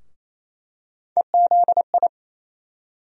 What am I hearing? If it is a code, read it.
Morse code, heard as a keyed tone, about 28 wpm, 700 Hz. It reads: EZI
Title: USA Call Sign Postfixes - 3 Letters 28wpm